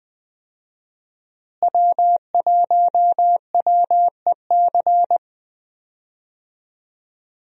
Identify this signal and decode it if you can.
Morse code, heard as a keyed tone, about 20 wpm, 700 Hz. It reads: W1WEC
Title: USA Call Signs - 5 Characters - Repeated 3x - 20wpm